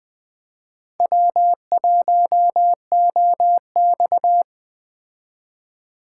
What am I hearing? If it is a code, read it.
Morse code, heard as a keyed tone, about 20 wpm, 700 Hz. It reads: W1OX
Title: Speed Racing - USA 1x2 Call Signs 17wpm